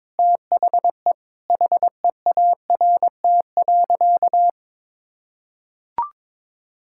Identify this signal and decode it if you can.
Morse code, heard as a keyed tone, about 21 wpm, 700 Hz. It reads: THE HEART.
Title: Sentences from Top 1600 Words 22wpm